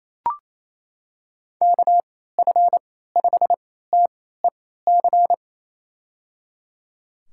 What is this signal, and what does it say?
Morse code, heard as a keyed tone, about 28 wpm, spaced as for 16 wpm, 700 Hz. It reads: KF5TEC
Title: USA Call Signs - 6 Characters - 3x Character Spacing - 28wpm